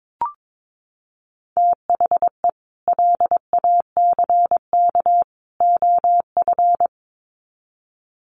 Morse code, heard as a keyed tone, about 22 wpm, 700 Hz. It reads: THE LACK OF
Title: Sets of 3 Words 22wpm